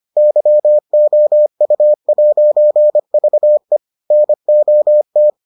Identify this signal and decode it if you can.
Morse code, heard as a keyed tone, about 25 wpm, 600 Hz. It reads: YOU'VE NOT